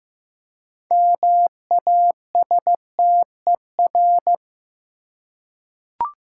Morse code, heard as a keyed tone, about 15 wpm, 700 Hz. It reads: MASTER
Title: Six Letter Words - 15wpm